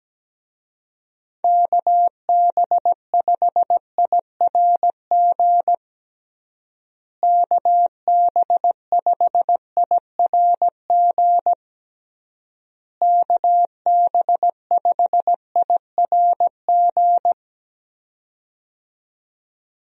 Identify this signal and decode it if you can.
Morse code, heard as a keyed tone, about 17 wpm, 700 Hz. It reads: KB5IRG KB5IRG KB5IRG
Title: USA Call Signs - 6 Characters - Repeated 3x - 17wpm